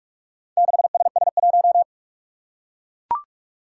Morse code, heard as a keyed tone, about 43 wpm, 700 Hz. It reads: 6SS1
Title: Four Challenging Letter-Number Combinations 45wpm